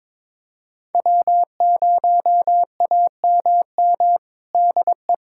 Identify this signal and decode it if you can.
Morse code, heard as a keyed tone, about 22 wpm, 700 Hz. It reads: W0AMM DE
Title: USA Callsign DE Callsign 22wpm